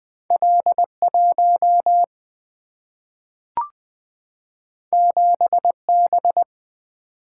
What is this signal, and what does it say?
Morse code, heard as a keyed tone, about 20 wpm, 700 Hz. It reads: L 1 7 B
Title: Single-Letter Single-Number 20wpm